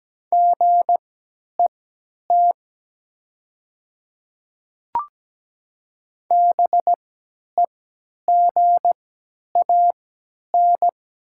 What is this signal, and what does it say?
Morse code, heard as a keyed tone, about 17 wpm, spaced as for 10 wpm, 700 Hz. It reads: GET BEGAN
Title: 3x Character Spacing - Top 500 Words - 17wpm